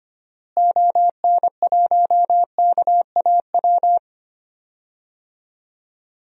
Morse code, heard as a keyed tone, about 25 wpm, 700 Hz. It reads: ON1KAW
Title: DXCC-100 Call Signs - 6 Characters - Repeated 3x - 25wpm